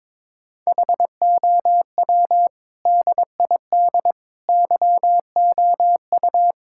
Morse code, heard as a keyed tone, about 22 wpm, 700 Hz. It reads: HOW DID YOU
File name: practice / 22wpm / Sets of 7 Words - 22wpm